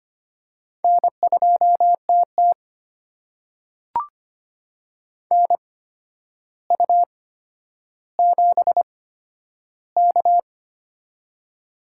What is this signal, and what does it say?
Morse code, heard as a keyed tone, about 25 wpm, spaced as for 11 wpm, 700 Hz. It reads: N2TT NU7K
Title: USA Call Signs - 8x Character Spacing - 25wpm